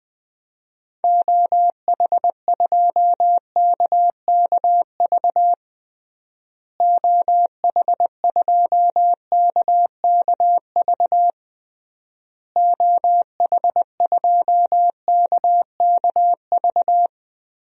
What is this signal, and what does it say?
Morse code, heard as a keyed tone, about 20 wpm, 700 Hz. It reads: OH2KKV OH2KKV OH2KKV
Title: DXCC-100 Call Signs - 6 Characters - Repeated 3x - 20wpm